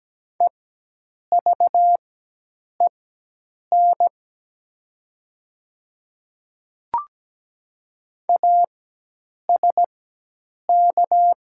Morse code, heard as a keyed tone, about 17 wpm, spaced as for 8 wpm, 700 Hz. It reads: EVEN ASK
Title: 4x Character Spacing - Top 300 Words - 17wpm